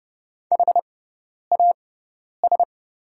Morse code, heard as a keyed tone, about 30 wpm, spaced as for 10 wpm, 700 Hz. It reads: HAS
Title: 6x Character Spacing - Top 200 Words - 30wpm